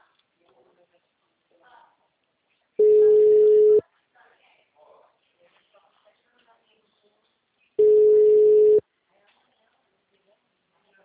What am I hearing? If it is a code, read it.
Ringback tone (Europe, 425 Hz), with speech.